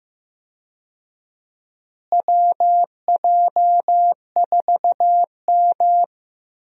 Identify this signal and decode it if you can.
Morse code, heard as a keyed tone, about 15 wpm, 700 Hz. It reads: WJ4M
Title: USA Call Signs - 4 Characters - Repeated 3x - 15wpm